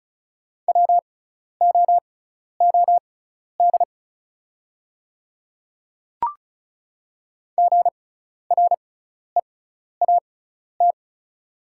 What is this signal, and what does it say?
Morse code, heard as a keyed tone, about 36 wpm, spaced as for 12 wpm, 700 Hz. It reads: WOOD GREAT
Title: 6x Character Spacing - Top 500 Words - 35wpm